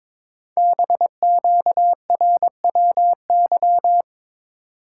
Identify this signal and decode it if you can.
Morse code, heard as a keyed tone, about 23 wpm, 700 Hz. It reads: BQRWY
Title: Code Groups - 5 Letters - 22wpm